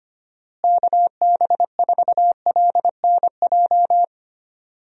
Morse code, heard as a keyed tone, about 25 wpm, 700 Hz. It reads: KB4LNJ
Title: USA Call Signs - 6 Characters - Repeated 3x - 25wpm